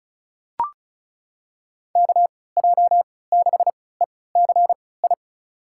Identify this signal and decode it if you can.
Morse code, heard as a keyed tone, about 36 wpm, spaced as for 20 wpm, 700 Hz. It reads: KJ6ECI
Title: USA Call Signs - 6 Characters - 3x Character Spacing - 35wpm